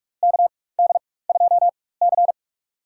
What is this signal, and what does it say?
Morse code, heard as a keyed tone, about 42 wpm, spaced as for 21 wpm, 700 Hz. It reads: KD2C
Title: USA Call Signs - 4 Characters - 4x Character Spacing - 45wpm